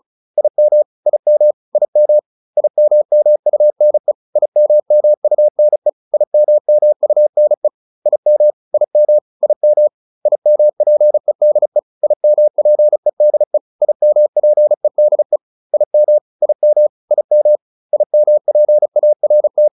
Morse code, heard as a keyed tone, about 36 wpm, 600 Hz. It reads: IM IM IM IMMUNE IMMUNE IMMUNE IM IM IM IMPEDE IMPEDE IMPEDE IM IM IM IMPART